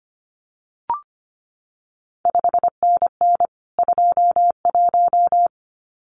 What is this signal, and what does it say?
Morse code, heard as a keyed tone, about 25 wpm, 700 Hz. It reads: 5NN 21
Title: CQ WW DX Contest Exchange Encore - 25wpm